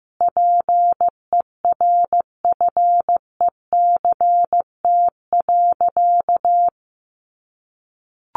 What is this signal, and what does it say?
Morse code, heard as a keyed tone, about 14 wpm, 700 Hz. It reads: PERFECT.
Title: Sentences from Top 1500 Words 15wpm